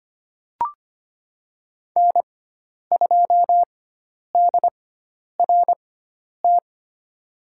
Morse code, heard as a keyed tone, about 25 wpm, spaced as for 10 wpm, 700 Hz. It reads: N2DRT
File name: Lesson 16g - Letter c - Call Signs 25wpm